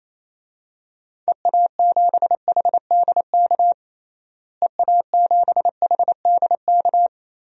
Morse code, heard as a keyed tone, about 28 wpm, 700 Hz. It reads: EA7HDK EA7HDK
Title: DXCC-100 Call Signs - 6 Characters - Repeated 3x - 28wpm